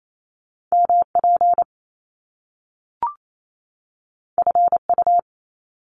Morse code, heard as a keyed tone, about 28 wpm, 700 Hz. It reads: MP FU
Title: Top Two Letter Combinations in Words 28wpm